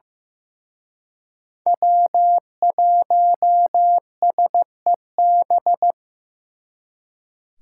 Morse code, heard as a keyed tone, about 18 wpm, 700 Hz. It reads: W1SEB
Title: Speed Racing - USA 1x3 Call Signs 15wpm